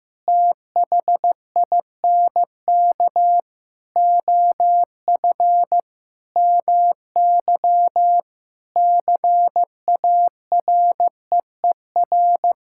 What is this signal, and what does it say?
Morse code, heard as a keyed tone, about 15 wpm, 700 Hz. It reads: THINK OF MY CAREER
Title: Sentences from Top 1000 Words - Review Part 6 - 15wpm